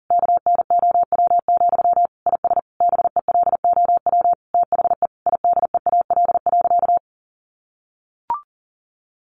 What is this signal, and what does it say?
Morse code, heard as a keyed tone, about 40 wpm, 700 Hz. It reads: KNOW, IS BELOW THE IDEAL.
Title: Sentences from Top 1400 Words 40wpm